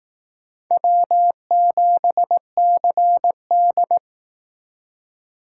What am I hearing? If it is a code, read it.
Morse code, heard as a keyed tone, about 18 wpm, 700 Hz. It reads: W7CD
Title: Speed Racing - USA 1x2 Call Signs 15wpm